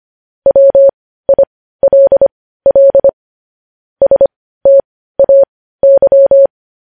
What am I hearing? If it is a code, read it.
Morse code, heard as a keyed tone, about 25 wpm, spaced as for 15 wpm, 550 Hz. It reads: WILL STAY
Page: News Headlines in Morse Code at 15 WPM